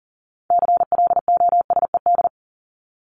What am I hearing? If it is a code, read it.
Morse code, heard as a keyed tone, about 40 wpm, 700 Hz. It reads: CLOSED
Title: ED Suffix - Top 100 Words with Base Words 40wpm